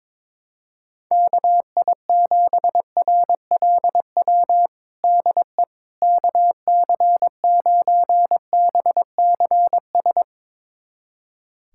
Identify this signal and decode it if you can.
Morse code, heard as a keyed tone, about 23 wpm, 700 Hz. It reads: KI7RLW DE KC9BCS
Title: USA Callsign DE Callsign - 6 Characters - 22wpm